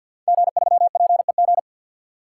Morse code, heard as a keyed tone, about 50 wpm, 700 Hz. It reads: G3PEG
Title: Code Groups - 5 Letter:Numbers - 50wpm